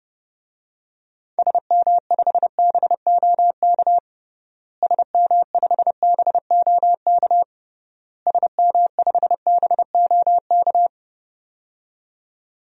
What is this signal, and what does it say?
Morse code, heard as a keyed tone, about 30 wpm, 700 Hz. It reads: SM5BOK SM5BOK SM5BOK
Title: DXCC-100 Call Signs - 6 Characters - Repeated 3x - 30wpm